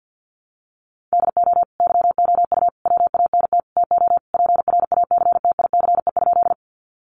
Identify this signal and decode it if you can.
Morse code, heard as a keyed tone, about 50 wpm, 700 Hz. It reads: DO YOU WANT TO PRACTICE?
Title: Sentences from Top 700 Words 50wpm